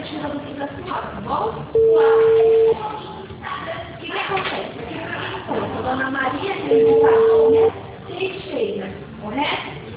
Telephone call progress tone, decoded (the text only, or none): ringback tone